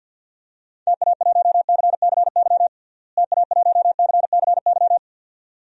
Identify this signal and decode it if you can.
Morse code, heard as a keyed tone, about 47 wpm, 700 Hz. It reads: TA1CCY TA1CCY
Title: DXCC-100 Call Signs - 6 Characters - Repeated 2x - 50wpm